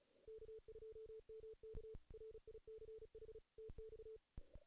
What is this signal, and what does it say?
Morse code, heard as a keyed tone, about 34 wpm, 440 Hz. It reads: K2MK RICH TX
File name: 2016 JAN CW NAQP / 2016 Jan naqp k5ot - 0-2